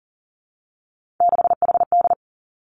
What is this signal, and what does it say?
Morse code, heard as a keyed tone, about 42 wpm, 700 Hz. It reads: 6HD
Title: Three Challenging Letter-Number Combinations 40wpm